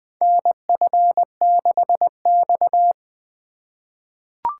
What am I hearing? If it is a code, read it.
Morse code, heard as a keyed tone, about 20 wpm, 700 Hz. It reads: NF6X
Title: USA Call Signs - 2x Character Spacing - 20wpm